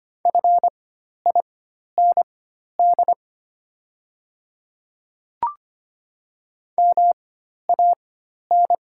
Morse code, heard as a keyed tone, about 25 wpm, spaced as for 12 wpm, 700 Hz. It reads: FIND MAN
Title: 4x Character Spacing - Top 200 Words - 25wpm